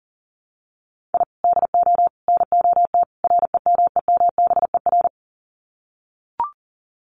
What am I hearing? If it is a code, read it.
Morse code, heard as a keyed tone, about 38 wpm, 700 Hz. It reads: I DO NOT REMEMBER
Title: Sets of 4 Words 40wpm